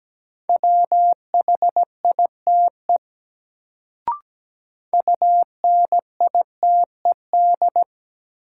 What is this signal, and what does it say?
Morse code, heard as a keyed tone, about 17 wpm, 700 Hz. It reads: WHITE UNITED
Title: Sets of 2 Words - New Word Drill - 17wpm